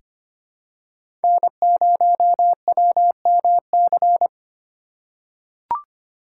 Morse code, heard as a keyed tone, about 25 wpm, 700 Hz. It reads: N0WMC
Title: USA Call Signs - 5 Characters - 4x Character Spacing - 25wpm